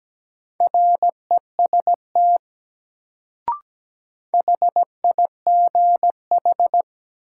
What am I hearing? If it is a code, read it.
Morse code, heard as a keyed tone, about 17 wpm, 700 Hz. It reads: REST HIGH
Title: Sets of 2 Words - New Word Drill - 17wpm